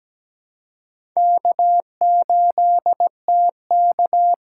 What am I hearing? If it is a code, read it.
Morse code, heard as a keyed tone, about 17 wpm, 700 Hz. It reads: K8TK
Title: USA Call Signs - 6x Character Spacing - 17wpm